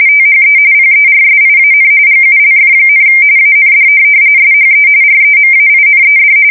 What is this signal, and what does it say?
RTTY, about 45.45 baud. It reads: THE PERSIAN GULF, AND SHE REMAINED
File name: fsk170 / USS-New-Jersey